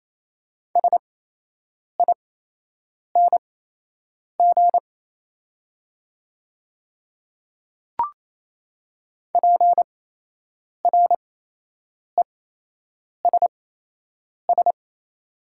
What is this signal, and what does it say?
Morse code, heard as a keyed tone, about 28 wpm, spaced as for 8 wpm, 700 Hz. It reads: SING PRESS
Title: 8x Character Spacing - Top 500 Words - 28wpm